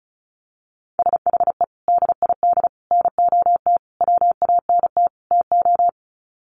Morse code, heard as a keyed tone, about 34 wpm, 700 Hz. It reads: SHE DID NOT WANT TO
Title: Sets of 5 Words 35wpm